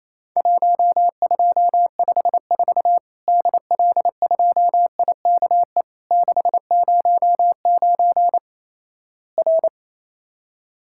Morse code, heard as a keyed tone, about 28 wpm, 700 Hz. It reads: 1254 DL2IKE 609 R
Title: Worked All Europe DX Contest - QTC Exchange - 28wpm